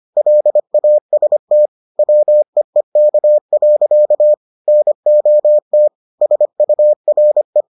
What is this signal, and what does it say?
Morse code, heard as a keyed tone, about 25 wpm, 600 Hz. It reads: LAST WEEK. NOT SURE